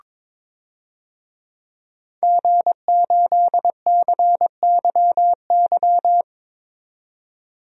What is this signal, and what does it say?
Morse code, heard as a keyed tone, about 22 wpm, 700 Hz. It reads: G8CYY
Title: DXCC-100 Call Signs - 5 Characters - 22wpm